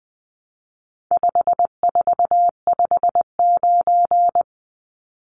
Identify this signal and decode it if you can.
Morse code, heard as a keyed tone, about 20 wpm, 700 Hz. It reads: 5459
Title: Numbers - 4-Digit 20wpm